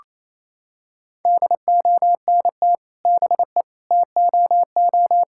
Morse code, heard as a keyed tone, about 28 wpm, 700 Hz. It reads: DONT BE TOO
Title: Sentences from Top 1000 Words - Review Part 5 - 28wpm